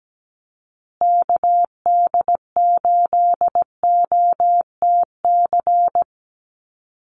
Morse code, heard as a keyed tone, about 17 wpm, 700 Hz. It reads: KD8OTC
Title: USA Call Signs Repeated 3x 17wpm